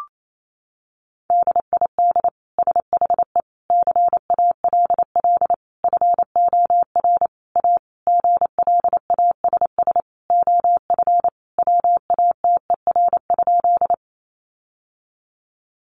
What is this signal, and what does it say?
Morse code, heard as a keyed tone, about 28 wpm, 700 Hz. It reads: DID SHE CALL FOR A GLASS OF WATER?
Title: Sentences from Top 1300 Words 28wpm